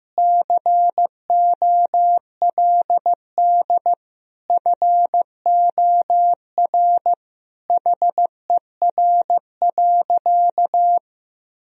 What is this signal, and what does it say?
Morse code, heard as a keyed tone, about 15 wpm, 700 Hz. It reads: COLD FOR HER.
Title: Sentences from Top 1000 Words - Review Part 4 - 15wpm